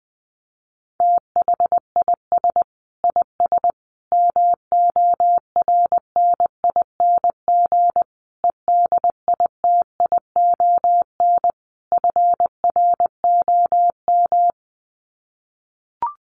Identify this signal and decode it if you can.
Morse code, heard as a keyed tone, about 20 wpm, 700 Hz. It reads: THIS IS MORNING EDITION FROM
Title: Sets of 5 Words Encore2 20wpm